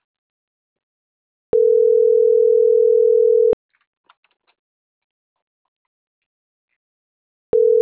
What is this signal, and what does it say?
Ringback tone (North America), 440 + 480 Hz.